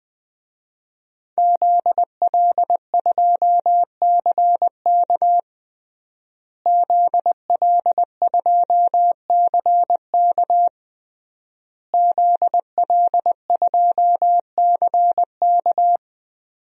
Morse code, heard as a keyed tone, about 20 wpm, 700 Hz. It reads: ZL2CK ZL2CK ZL2CK
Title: DXCC-100 Call Signs - 5 Characters - Repeated 3x - 20wpm